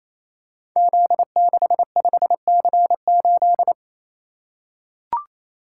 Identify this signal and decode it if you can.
Morse code, heard as a keyed tone, about 27 wpm, 700 Hz. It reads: Z65C8
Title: Code Groups - 5 Letter:Numbers - 28wpm